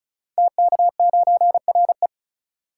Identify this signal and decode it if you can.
Morse code, heard as a keyed tone, about 36 wpm, 690 Hz. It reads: TK9RE
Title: Code Groups - 5 Letter:Numbers - 35wpm